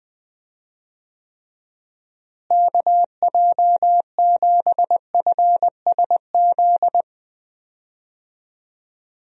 Morse code, heard as a keyed tone, about 20 wpm, 700 Hz. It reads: KJ7FSZ
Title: USA Call Signs - 6 Characters - Repeated 2x - 20wpm